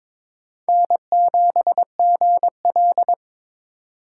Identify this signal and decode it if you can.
Morse code, heard as a keyed tone, about 26 wpm, 700 Hz. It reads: N7GL
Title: Speed Racing - USA 1x2 Call Signs 22wpm